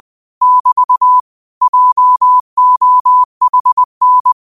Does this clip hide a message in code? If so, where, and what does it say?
Yes: Morse code from 0.4 to 4.3 s, heard as a keyed tone, about 20 wpm, 1000 Hz. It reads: <BT> JOHN